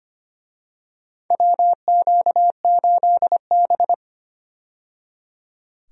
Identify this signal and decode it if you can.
Morse code, heard as a keyed tone, about 25 wpm, 700 Hz. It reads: WQ8B
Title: USA Call Signs - 4 Characters - 25wpm